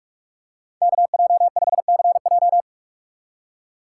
Morse code, heard as a keyed tone, about 45 wpm, 700 Hz. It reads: KJ5CJ